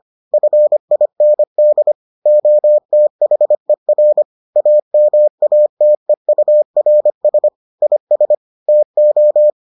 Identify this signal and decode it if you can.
Morse code, heard as a keyed tone, about 25 wpm, 600 Hz. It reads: FIND OTHER AMATEURS IS TO